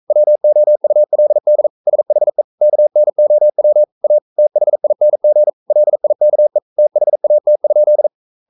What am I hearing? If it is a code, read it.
Morse code, heard as a keyed tone, about 42 wpm, 600 Hz. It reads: WOULD SHE KNOW A THING LIKE THAT?
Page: Romanen Never af Ken Follet på engelsk.